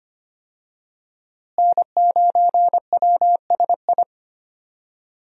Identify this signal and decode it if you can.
Morse code, heard as a keyed tone, about 25 wpm, 700 Hz. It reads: N9WSI
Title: USA Call Signs - 5 Characters - Repeated 3x - 25wpm